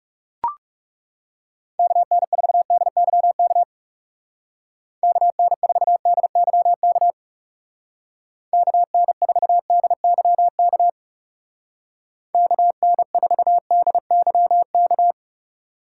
Morse code, heard as a keyed tone, about 35 wpm, 700 Hz. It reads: KN4DYK KN4DYK KN4DYK KN4DYK
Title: Speed Racing - USA Call Signs 30wpm